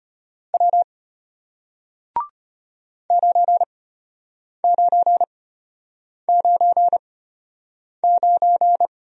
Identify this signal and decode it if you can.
Morse code, heard as a keyed tone, about 34 wpm, spaced as for 8 wpm, 700 Hz. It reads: W 9999